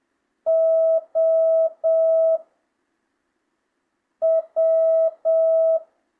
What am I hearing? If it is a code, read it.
Morse code, heard as a keyed tone, about 7 wpm, 660 Hz. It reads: O W
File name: jh8cbh / 2025cw